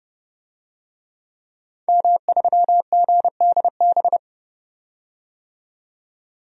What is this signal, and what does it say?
Morse code, heard as a keyed tone, about 30 wpm, 700 Hz. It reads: M3GDB
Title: DXCC-100 Call Signs - 5 Characters - 30wpm